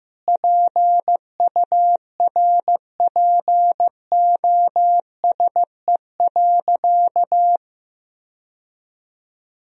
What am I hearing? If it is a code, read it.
Morse code, heard as a keyed tone, about 15 wpm, 700 Hz. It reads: PURPOSE.
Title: Sentences from Top 1000 Words - Review Part 2 - 15wpm